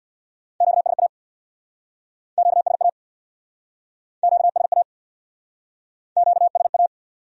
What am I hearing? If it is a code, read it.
Morse code, heard as a keyed tone, about 69 wpm, 700 Hz. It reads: QSA QSA QSA QSA